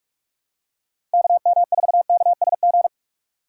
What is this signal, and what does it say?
Morse code, heard as a keyed tone, about 46 wpm, 690 Hz. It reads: KM4KSG
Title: USA Call Signs - 6 Characters - 3x Character Spacing - 45wpm